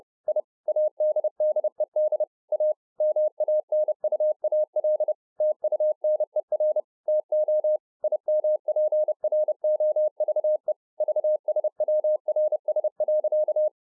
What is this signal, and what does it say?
Morse code, heard as a keyed tone, about 30 wpm, 610 Hz. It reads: I ADDED A MANUAL TUNER TO IMPROVE VSWRS.